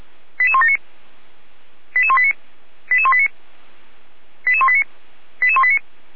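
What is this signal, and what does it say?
Five-tone selective call (ZVEI), heard as quick runs of five tones, tones of about 70 ms. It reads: 88179 88189 88189 88189 88189
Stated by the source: Received on SDR as a ZVEI five-tone selective call.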